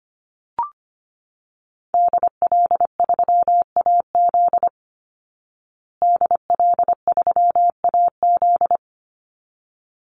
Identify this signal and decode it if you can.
Morse code, heard as a keyed tone, about 25 wpm, 700 Hz. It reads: DL3AZ DL3AZ
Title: Call Signs Repeated 3x - 25wpm